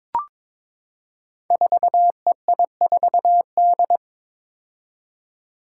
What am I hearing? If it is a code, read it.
Morse code, heard as a keyed tone, about 22 wpm, 700 Hz. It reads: 4EI4D
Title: Five Challenging Letter-Number Combinations 22wpm